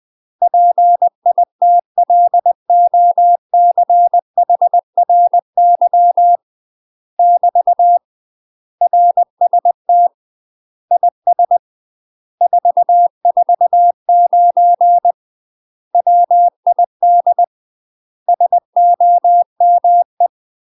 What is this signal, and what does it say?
Morse code, heard as a keyed tone, about 20 wpm, 700 Hz. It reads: PITLOCHRY <BT> RST IS 449 WID SOME
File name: UK MASTER QSOs TRAINING / 20WPM / UK MASTER QSOs 0048